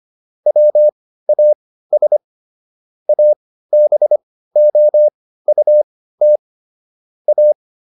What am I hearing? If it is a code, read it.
Morse code, heard as a keyed tone, about 25 wpm, spaced as for 15 wpm, 600 Hz. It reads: WAS ABOUT A
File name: SS101 15